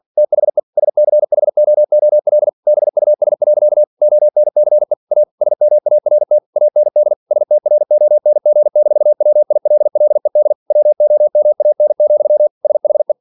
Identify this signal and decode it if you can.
Morse code, heard as a keyed tone, about 46 wpm, 600 Hz. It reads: THE SCHOOL BUS. ONCE A SMART AND STRONG-WILLED WOMAN, SHE
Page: Never 47wpm